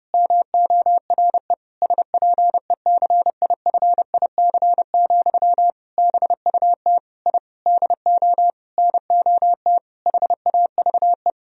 Morse code, heard as a keyed tone, about 30 wpm, 700 Hz. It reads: MORE SPECIFIC, BUT I DO NOT HAVE
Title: Sentences from Top 1000 Words - Review Part 3 - 30wpm